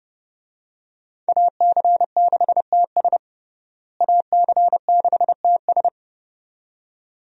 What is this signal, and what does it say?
Morse code, heard as a keyed tone, about 32 wpm, 700 Hz. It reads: AC6TS AC6TS
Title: USA Call Signs - 5 Characters - Repeated 2x - 30wpm